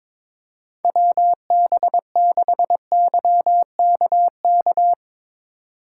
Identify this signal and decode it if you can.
Morse code, heard as a keyed tone, about 22 wpm, 700 Hz. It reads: WB6YKK
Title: USA Call Signs - 6 Characters - 4x Character Spacing - 22wpm